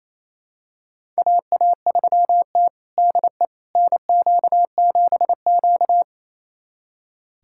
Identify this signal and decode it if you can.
Morse code, heard as a keyed tone, about 27 wpm, 700 Hz. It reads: AA3T DE NQ7Q
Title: USA Callsign DE Callsign - 4 Characters - 28wpm